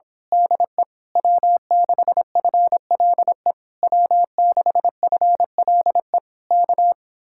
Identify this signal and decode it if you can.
Morse code, heard as a keyed tone, about 26 wpm, 700 Hz. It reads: DE W6FLE W6FLE K
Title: Speed-Racing Common QSO Phrases - 22wpm